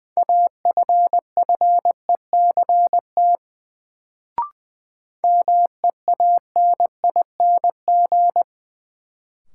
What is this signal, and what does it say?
Morse code, heard as a keyed tone, about 21 wpm, 700 Hz. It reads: AFFECT MEANING
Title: Sentences from Top 900 Words - New Word Drill - 20wpm